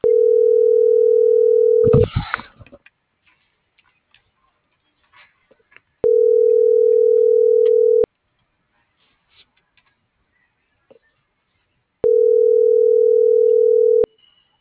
Ringback tone (North America), 440 + 480 Hz.
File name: CENTRAL SOTOVA / out alejandro.rodriguez - .56674